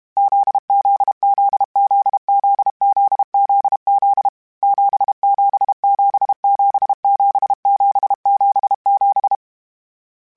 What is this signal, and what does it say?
Morse code, heard as a keyed tone, about 32 wpm, 800 Hz. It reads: ZZZZZZZZ 77777777